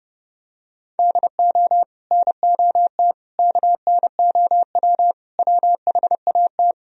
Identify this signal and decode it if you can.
Morse code, heard as a keyed tone, about 28 wpm, 700 Hz. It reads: DO NOT KNOW WHAT